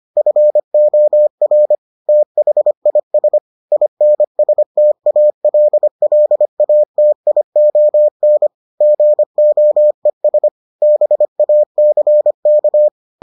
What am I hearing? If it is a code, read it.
Morse code, heard as a keyed tone, about 25 wpm, 600 Hz. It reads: FOR THIS INSTALLATION GOES BACK